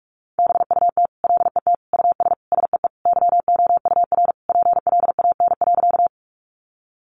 Morse code, heard as a keyed tone, about 44 wpm, 700 Hz. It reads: BUT LET US SEE YOUR PLAN.
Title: Sentences from Top 500 Words 45wpm